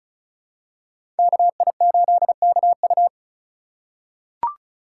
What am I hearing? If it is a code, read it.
Morse code, heard as a keyed tone, about 34 wpm, 700 Hz. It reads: KI8KU
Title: USA Call Signs - 2x Character Spacing - 35wpm